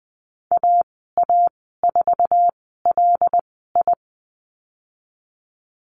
Morse code, heard as a keyed tone, about 20 wpm, spaced as for 15 wpm, 700 Hz. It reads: AA4LI
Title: 082 Lesson 14g - Number 2 - Call Signs 20wpm